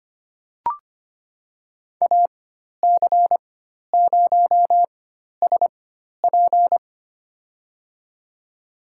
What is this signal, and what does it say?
Morse code, heard as a keyed tone, about 25 wpm, spaced as for 12 wpm, 700 Hz. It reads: AC0SP
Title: USA Call Signs - 4x Character Spacing - 25wpm